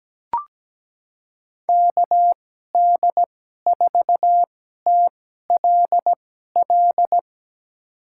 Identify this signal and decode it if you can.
Morse code, heard as a keyed tone, about 17 wpm, spaced as for 12 wpm, 700 Hz. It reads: KD4TLL
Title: USA Call Signs - 6 Characters - 2x Character Spacing - 17wpm